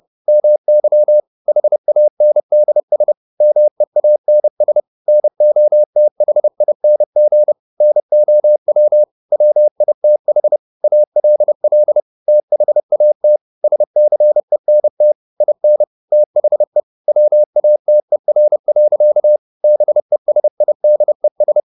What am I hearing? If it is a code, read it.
Morse code, heard as a keyed tone, about 30 wpm, 600 Hz. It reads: MY HANDS MEANS NOTHING NOW WITH ALL THAT SCENT IN THE WATER. BESIDES